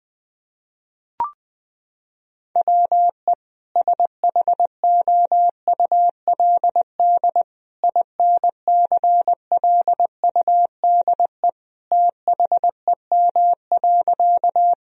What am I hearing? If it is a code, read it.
Morse code, heard as a keyed tone, about 20 wpm, 700 Hz. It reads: WE SHOULD INCLUDE THEM.
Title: Sentences from Top 1000 Words - Review Part 3 - 20wpm